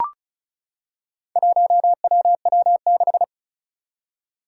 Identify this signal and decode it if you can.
Morse code, heard as a keyed tone, about 36 wpm, 700 Hz. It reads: 1WW6
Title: Four Challenging Letter-Number Combinations 35wpm